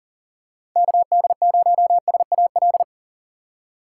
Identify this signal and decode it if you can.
Morse code, heard as a keyed tone, about 37 wpm, 700 Hz. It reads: KD0SAL
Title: USA Call Signs - 6 Characters - 3x Character Spacing - 40wpm